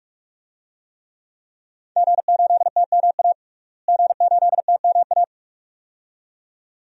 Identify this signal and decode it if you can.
Morse code, heard as a keyed tone, about 44 wpm, 700 Hz. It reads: G8TMA G8TMA
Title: DXCC-100 Call Signs - 5 Characters - Repeated 2x - 45wpm